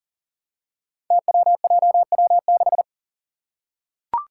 Morse code, heard as a keyed tone, about 41 wpm, 700 Hz. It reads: TWJW6
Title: Five Challenging Letter-Number Combinations 40wpm